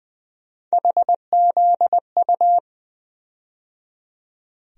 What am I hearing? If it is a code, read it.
Morse code, heard as a keyed tone, about 20 wpm, 700 Hz. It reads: HZU
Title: USA Call Sign Postfixes - 3 Letters 20wpm